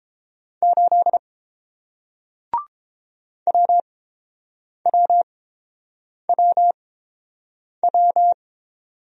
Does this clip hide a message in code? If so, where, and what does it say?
Yes: Morse code from 0.6 to 8.3 s, heard as a keyed tone, about 30 wpm, spaced as for 7 wpm, 700 Hz. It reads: 8 WWWW